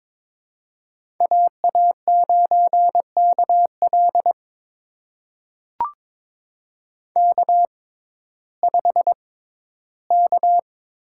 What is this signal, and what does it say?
Morse code, heard as a keyed tone, about 22 wpm, spaced as for 13 wpm, 700 Hz. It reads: AA9KL K5K
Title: USA Call Signs - 5 Characters - 6x Character Spacing - 22wpm